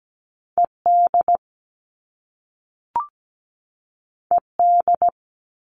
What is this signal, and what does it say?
Morse code, heard as a keyed tone, about 17 wpm, 700 Hz. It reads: ED ED
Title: All Cut Numbers - 2-Digit 17wpm